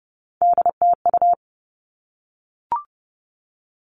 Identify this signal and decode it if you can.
Morse code, heard as a keyed tone, about 30 wpm, 700 Hz. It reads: DTU
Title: All Cut Numbers - 3-Digit 30wpm